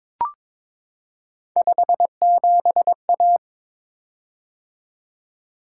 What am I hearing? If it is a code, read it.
Morse code, heard as a keyed tone, about 22 wpm, 700 Hz. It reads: 57A
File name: practice / 22wpm / Three Letter-Number Combinations - 22wpm